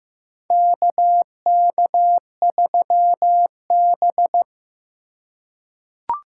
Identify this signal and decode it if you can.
Morse code, heard as a keyed tone, about 15 wpm, 700 Hz. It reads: KK3B
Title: USA Call Signs - 4 Characters - 3x Character Spacing - 15wpm